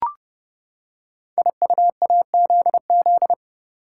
Morse code, heard as a keyed tone, about 30 wpm, 700 Hz. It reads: IUAZZ
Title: Code Groups - 5 Letters - 30wpm